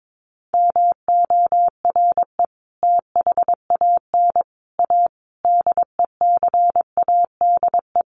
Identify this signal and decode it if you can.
Morse code, heard as a keyed tone, about 22 wpm, 700 Hz. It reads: MORE THAN A DECADE
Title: Sets of 5 Words Encore2 22wpm